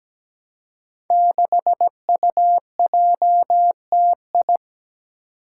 Morse code, heard as a keyed tone, about 17 wpm, 700 Hz. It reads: 6UJTI
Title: Five Challenging Letter-Number Combinations 17wpm